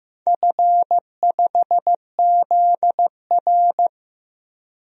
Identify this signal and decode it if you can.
Morse code, heard as a keyed tone, about 15 wpm, 700 Hz. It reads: F5ZR
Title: DXCC-100 Call Signs - 4 Characters - 15wpm